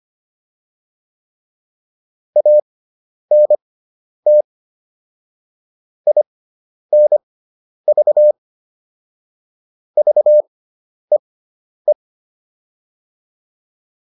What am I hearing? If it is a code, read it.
Morse code, heard as a keyed tone, about 25 wpm, spaced as for 10 wpm, 600 Hz. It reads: ANT INV VEE